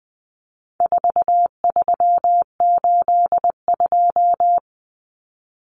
Morse code, heard as a keyed tone, about 20 wpm, 700 Hz. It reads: 4382
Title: Common Cut Numbers - 4-Digit 20wpm